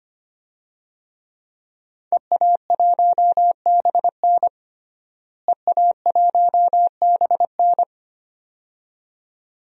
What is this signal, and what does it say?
Morse code, heard as a keyed tone, about 25 wpm, 700 Hz. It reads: EA1BN EA1BN
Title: DXCC-100 Call Signs - 5 Characters - Repeated 2x - 25wpm